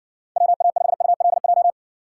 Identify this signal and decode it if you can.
Morse code, heard as a keyed tone, about 60 wpm, 700 Hz. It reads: WA5ULJ